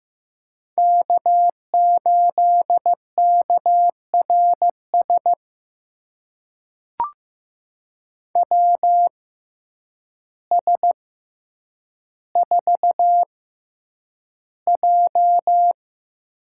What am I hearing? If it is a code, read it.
Morse code, heard as a keyed tone, about 15 wpm, spaced as for 8 wpm, 700 Hz. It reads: K8KRS WS4J
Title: USA Call Signs - 5 Characters - 6x Character Spacing - 15wpm